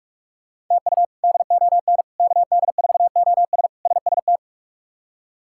Morse code, heard as a keyed tone, about 46 wpm, 700 Hz. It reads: TU DON KD4OS SST
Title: K1USN SST QSO Practice - 45wpm